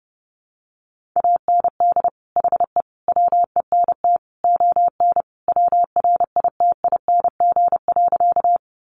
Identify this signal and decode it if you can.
Morse code, heard as a keyed tone, about 30 wpm, 700 Hz. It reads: AND HE WENT ON WRITING.
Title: Sentences from Top 800 Words 30wpm